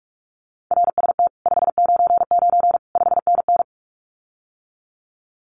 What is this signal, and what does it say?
Morse code, heard as a keyed tone, about 44 wpm, 700 Hz. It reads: RST 599 5NN
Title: Common QSO Phrases Encore 45wpm